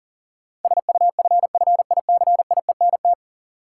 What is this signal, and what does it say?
Morse code, heard as a keyed tone, about 40 wpm, 700 Hz. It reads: SUFFICIENT
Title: Sentences from Top 1400 Words - New Word Drill Alt - 40wpm